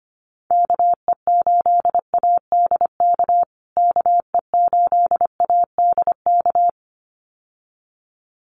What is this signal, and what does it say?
Morse code, heard as a keyed tone, about 25 wpm, 700 Hz. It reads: KE8ADK KE8ADK
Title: USA Call Signs Repeated 2x 25wpm